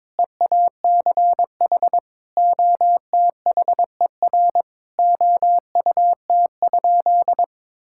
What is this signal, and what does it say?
Morse code, heard as a keyed tone, about 22 wpm, 700 Hz. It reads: EACH OTHER OUT?
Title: Sentences from Top 1000 Words - Review Part 6 - 22wpm